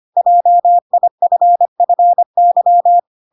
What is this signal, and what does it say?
Morse code, heard as a keyed tone, about 25 wpm, 700 Hz. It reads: JIFFY
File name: Pinocchio / Pinocchio ch5 25 25 ik1hge